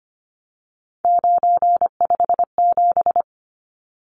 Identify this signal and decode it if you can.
Morse code, heard as a keyed tone, about 25 wpm, 700 Hz. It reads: 957
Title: Numbers - 3-Digit 25wpm